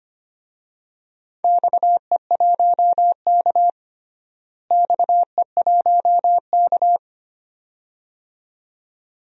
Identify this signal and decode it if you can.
Morse code, heard as a keyed tone, about 25 wpm, 700 Hz. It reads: XE1K XE1K
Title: DXCC-100 Call Signs - 4 Characters - Repeated 2x - 25wpm